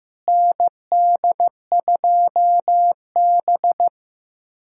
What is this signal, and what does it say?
Morse code, heard as a keyed tone, about 15 wpm, 700 Hz. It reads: ND2B
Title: USA Callsign DE Callsign - 4 Characters - 15wpm